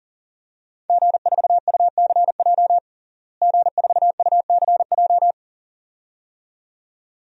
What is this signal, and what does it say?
Morse code, heard as a keyed tone, about 40 wpm, 700 Hz. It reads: G4UCJ G4UCJ
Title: DXCC-100 Call Signs - 5 Characters - Repeated 2x - 40wpm